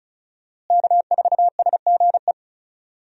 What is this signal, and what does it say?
Morse code, heard as a keyed tone, about 34 wpm, 700 Hz. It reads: K4SGE
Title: USA Call Signs - 5 Characters - Repeated 3x - 35wpm